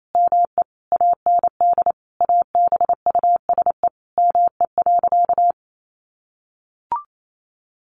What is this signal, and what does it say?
Morse code, heard as a keyed tone, about 28 wpm, 700 Hz. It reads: ME AND ABUSE ME.
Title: Sentences from Top 1300 Words 28wpm